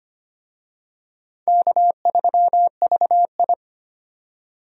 Morse code, heard as a keyed tone, about 25 wpm, 700 Hz. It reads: K3VI
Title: USA Call Signs - 4 Characters - 25wpm